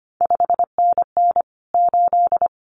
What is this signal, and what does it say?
Morse code, heard as a keyed tone, about 25 wpm, 700 Hz. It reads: 5NN 8
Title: CQ WW DX Contest Exchange - 25wpm